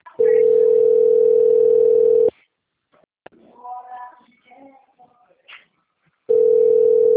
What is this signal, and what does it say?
Ringback tone (North America), 440 + 480 Hz.